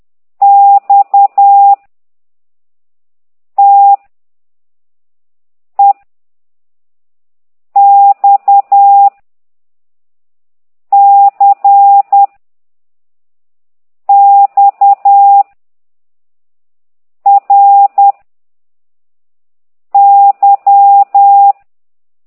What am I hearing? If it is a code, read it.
Morse code, heard as a keyed tone, about 10 wpm, 800 Hz. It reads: X T E X C X R Y
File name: exerc7